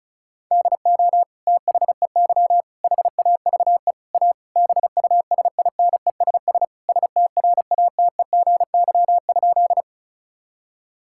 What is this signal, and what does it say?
Morse code, heard as a keyed tone, about 34 wpm, 700 Hz. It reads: DO THEY HAVE A BUSINESS STRATEGY?
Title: Sentences from Top 1000 Words - Review Part 5 - 35wpm